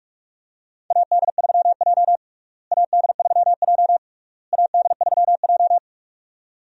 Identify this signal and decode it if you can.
Morse code, heard as a keyed tone, about 41 wpm, 700 Hz. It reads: AD3J AD3J AD3J
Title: USA Call Signs - 4 Characters - Repeated 3x - 45wpm